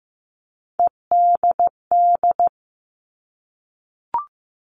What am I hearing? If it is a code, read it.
Morse code, heard as a keyed tone, about 15 wpm, 700 Hz. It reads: EDD
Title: All Cut Numbers - 3-Digit 15wpm